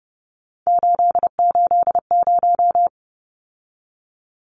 Morse code, heard as a keyed tone, about 30 wpm, 700 Hz. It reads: 880
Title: Numbers - 3-Digit 30wpm